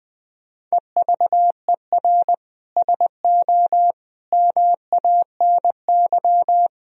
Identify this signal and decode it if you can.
Morse code, heard as a keyed tone, about 19 wpm, 700 Hz. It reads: EVER SO MANY
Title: Sentences from Top 1000 Words - Review Part 3 - 20wpm